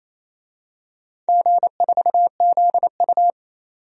Morse code, heard as a keyed tone, about 28 wpm, 700 Hz. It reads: G4ZU
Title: DXCC-100 Call Signs - 4 Characters - 28wpm